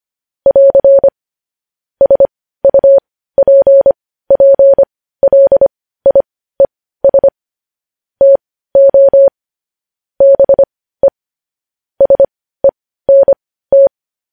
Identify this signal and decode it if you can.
Morse code, heard as a keyed tone, about 25 wpm, spaced as for 15 wpm, 550 Hz. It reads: <AR> SUPPLIES TO BE SENT
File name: podcast / News headlines-1x-25x15